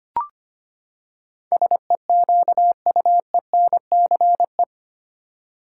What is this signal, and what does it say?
Morse code, heard as a keyed tone, about 25 wpm, 700 Hz. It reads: SEQUENCE
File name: Eight Letter Words - 25wpm